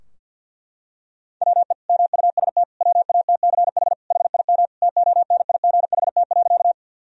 Morse code, heard as a keyed tone, about 50 wpm, 700 Hz. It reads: WE MUST WATCH HIM TONIGHT.
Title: Sentences from Top 1000 Words - Review Part 3 - 50wpm